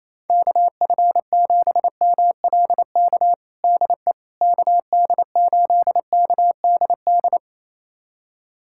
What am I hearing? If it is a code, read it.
Morse code, heard as a keyed tone, about 28 wpm, 700 Hz. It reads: KF7MLK DE KD8KDD
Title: USA Callsign DE Callsign 28wpm